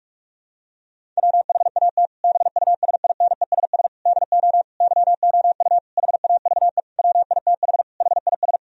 Morse code, heard as a keyed tone, about 42 wpm, 700 Hz. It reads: WHAT BUSINESS DO YOU HAVE WITH HIS
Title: Sentences from Top 1000 Words - Review Part 3 - 45wpm